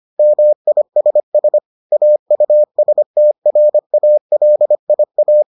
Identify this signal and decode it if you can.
Morse code, heard as a keyed tone, about 25 wpm, 600 Hz. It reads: MISS AUSTRALIA